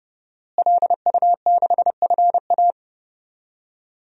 Morse code, heard as a keyed tone, about 30 wpm, 700 Hz. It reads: LU6FA
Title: DXCC-100 Call Signs - 5 Characters - Repeated 2x - 30wpm